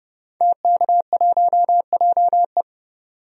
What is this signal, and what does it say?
Morse code, heard as a keyed tone, about 32 wpm, 700 Hz. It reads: TK1JE
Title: DXCC-100 Call Signs - 5 Characters - 30wpm